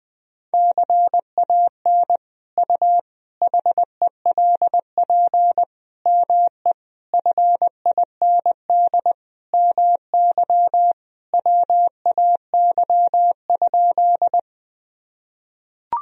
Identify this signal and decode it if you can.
Morse code, heard as a keyed tone, about 20 wpm, 700 Hz. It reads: CAN U HELP ME FIND MY WAY?
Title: Sentences from Top 200 Words and Abbreviations 20wpm